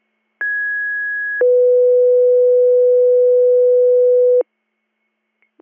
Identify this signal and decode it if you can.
Two-tone page, heard as a one-second tone, then a three-second tone, about 1650 then 500 Hz. It is heard from 0.4 to 4.4 s.